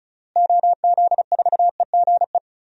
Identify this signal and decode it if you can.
Morse code, heard as a keyed tone, about 34 wpm, 700 Hz. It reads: OZ4EGE
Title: DXCC-100 Call Signs - 6 Characters - Repeated 3x - 35wpm